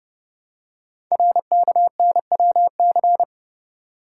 Morse code, heard as a keyed tone, about 30 wpm, 700 Hz. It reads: RKNWC
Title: Code Groups - 5 Letters - 30wpm